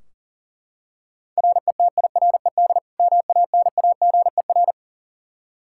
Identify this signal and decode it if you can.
Morse code, heard as a keyed tone, about 40 wpm, 700 Hz. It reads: RETIRED MANAGER
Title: Practice QSO - Exchange 3 - More Info - 40wpm